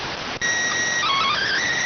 SSTV (slow-scan television), PD 290.